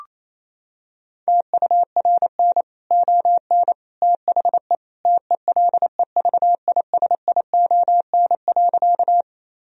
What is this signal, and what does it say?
Morse code, heard as a keyed tone, about 28 wpm, 700 Hz. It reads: TURN ON THE TELEVISION.
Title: Sentences from Top 1000 Words - Review Part 4 - 28wpm